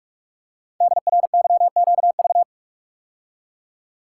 Morse code, heard as a keyed tone, about 44 wpm, 700 Hz. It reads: DRYQV
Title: Code Groups - 5 Letters - 45wpm